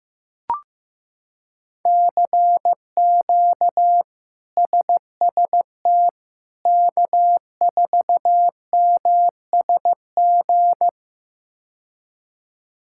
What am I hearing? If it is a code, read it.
Morse code, heard as a keyed tone, about 16 wpm, 700 Hz. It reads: CQ SST K4MSG
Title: K1USN SST QSO Practice - 15wpm